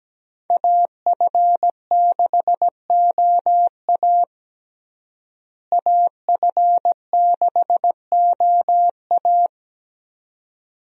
Morse code, heard as a keyed tone, about 17 wpm, 700 Hz. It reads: AF6OA AF6OA
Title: USA Call Signs - 5 Characters - Repeated 2x - 17wpm